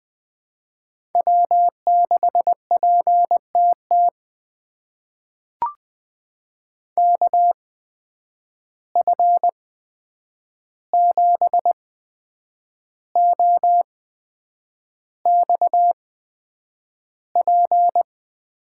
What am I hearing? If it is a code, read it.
Morse code, heard as a keyed tone, about 20 wpm, spaced as for 8 wpm, 700 Hz. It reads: W6PTT KF7OXP
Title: USA Call Signs - 8x Character Spacing - 20wpm